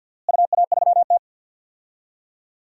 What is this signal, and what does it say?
Morse code, heard as a keyed tone, about 51 wpm, 700 Hz. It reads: UA3T
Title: DXCC-100 Call Signs - 4 Characters - Repeated 3x - 50wpm